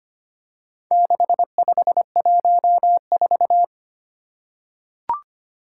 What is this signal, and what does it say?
Morse code, heard as a keyed tone, about 25 wpm, 700 Hz. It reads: 6514
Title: Four Challenging Letter-Number Combinations 25wpm